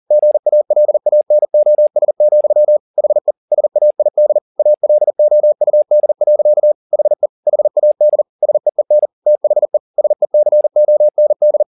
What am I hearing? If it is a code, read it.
Morse code, heard as a keyed tone, about 40 wpm, 600 Hz. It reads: GALANOS, HE SAID ALOUD. HE HAD SEEN THE SECOND
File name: the old man and the sea 2 40wpm